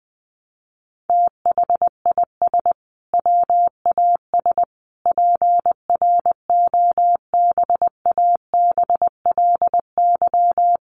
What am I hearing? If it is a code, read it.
Morse code, heard as a keyed tone, about 20 wpm, 700 Hz. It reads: THIS WAS PROBABLY
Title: Sentences from Top 700 Words 20wpm